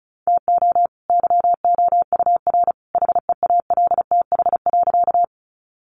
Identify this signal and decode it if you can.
Morse code, heard as a keyed tone, about 34 wpm, 700 Hz. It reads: TO YOUR HEALTH.
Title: Sentences from Top 600 Words 35wpm